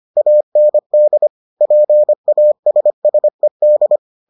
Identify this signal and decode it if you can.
Morse code, heard as a keyed tone, about 25 wpm, 600 Hz. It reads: AND PASSED